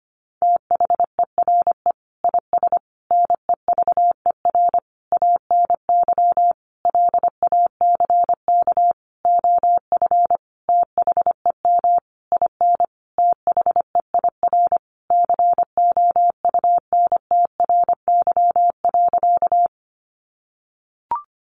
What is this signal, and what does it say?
Morse code, heard as a keyed tone, about 25 wpm, 700 Hz. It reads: THERE IS NEVER ANY LACK OF THEM IN THEIR COUNTRY.
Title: Sentences from Top 1100 Words 25wpm